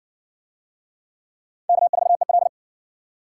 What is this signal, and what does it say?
Morse code, heard as a keyed tone, about 60 wpm, 700 Hz. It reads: K4EL